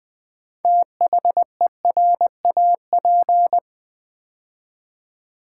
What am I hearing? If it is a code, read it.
Morse code, heard as a keyed tone, about 20 wpm, 700 Hz. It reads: THERAP
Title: Build A Word - 3 Letters At A Time - 9-Letter Words - 20wpm